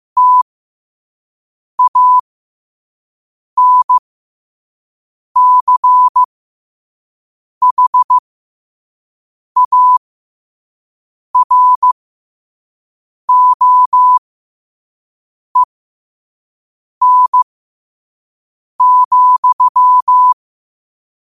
Morse code, heard as a keyed tone, about 15 wpm, spaced as for 5 wpm, 1000 Hz. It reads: TANCHAROEN,